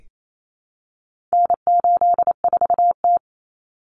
Morse code, heard as a keyed tone, about 28 wpm, 700 Hz. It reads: N84T
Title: Common Cut Numbers - 4-Digit 28wpm